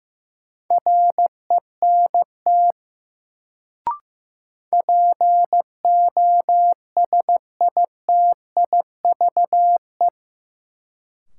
Morse code, heard as a keyed tone, about 15 wpm, 700 Hz. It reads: RENT POSITIVE
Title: Sentences from Top 1000 Words - Review Part 5 - New Word Drill - 15wpm